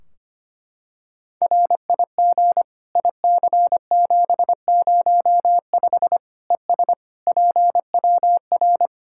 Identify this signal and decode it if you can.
Morse code, heard as a keyed tone, about 25 wpm, 700 Hz. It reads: RIG IC705 ES PWR
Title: Practice QSO - Exchange 2 - Info - 25wpm